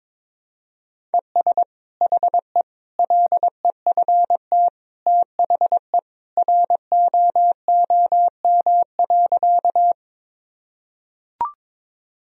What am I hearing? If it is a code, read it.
Morse code, heard as a keyed tone, about 21 wpm, 700 Hz. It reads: ES HE LEFT THE ROOM.
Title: Sentences from Top 300 Words and Abbreviations 22wpm